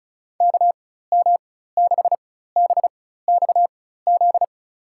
Morse code, heard as a keyed tone, about 34 wpm, spaced as for 16 wpm, 700 Hz. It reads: KM6BXZ
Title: USA Call Signs - 4x Character Spacing - 35wpm